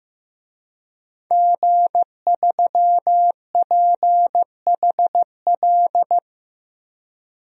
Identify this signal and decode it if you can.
Morse code, heard as a keyed tone, about 15 wpm, 700 Hz. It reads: G3PHL
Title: DXCC-100 Call Signs - 5 Characters - 15wpm